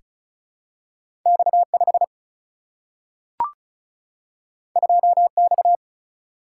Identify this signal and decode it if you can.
Morse code, heard as a keyed tone, about 36 wpm, 700 Hz. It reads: X 5 2 X
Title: Single-Letter Single-Number 35wpm